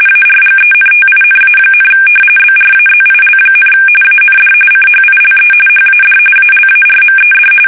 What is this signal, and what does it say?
RTTY, about 45.45 baud. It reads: ARGENTINA, BRAZIL, AND CHILE HAD ALARMED
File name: fsk850 / battleships